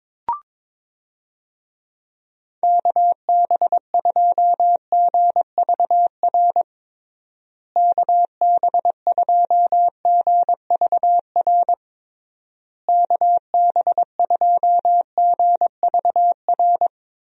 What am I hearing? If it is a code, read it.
Morse code, heard as a keyed tone, about 22 wpm, 700 Hz. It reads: KB2GVR KB2GVR KB2GVR
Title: USA Call Signs - 6 Characters - Repeated 3x - 22wpm